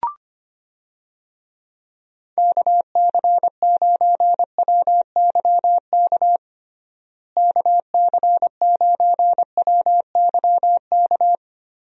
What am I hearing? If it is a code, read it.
Morse code, heard as a keyed tone, about 25 wpm, 700 Hz. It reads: KC9WYK KC9WYK
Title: USA Call Signs - 6 Characters - Repeated 2x - 25wpm